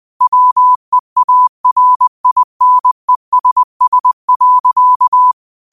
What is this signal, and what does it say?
Morse code, heard as a keyed tone, about 20 wpm, 1000 Hz. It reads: WEARINESS.
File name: qotdfiles / 20wpm